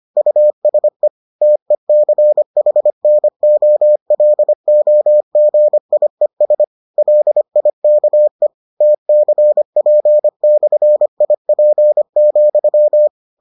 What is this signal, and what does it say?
Morse code, heard as a keyed tone, about 25 wpm, 600 Hz. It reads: USE TECHNOLOGIES LIKE TCP/IP,